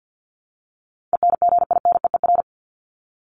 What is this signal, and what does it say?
Morse code, heard as a keyed tone, about 50 wpm, 700 Hz. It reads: ENGINEER
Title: ING Suffix - Top 100 Words with Base Words 50wpm